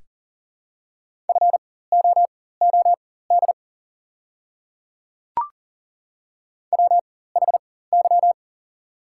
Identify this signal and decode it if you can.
Morse code, heard as a keyed tone, about 40 wpm, spaced as for 19 wpm, 700 Hz. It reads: FOOD WHY
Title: 4x Character Spacing - Top 300 Words - 40wpm